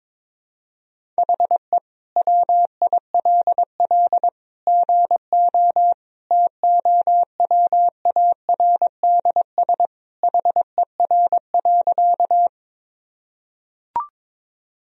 Morse code, heard as a keyed tone, about 22 wpm, 700 Hz. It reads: HE WILL GO TOWARDS HER.
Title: Sentences from Top 1000 Words - Review Part 2 - 22wpm